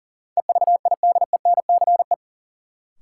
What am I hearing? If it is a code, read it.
Morse code, heard as a keyed tone, about 40 wpm, 700 Hz. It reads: EVIDENCE
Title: Sentences from Top 1300 Words - New Word Drill - 40wpm